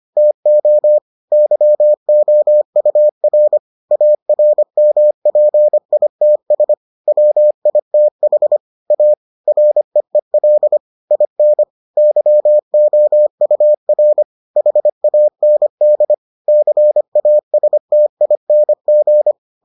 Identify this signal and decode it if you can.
Morse code, heard as a keyed tone, about 25 wpm, 600 Hz. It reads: TO YOUR ARMPITS WITH A REEL IN YOUR HAND CASTING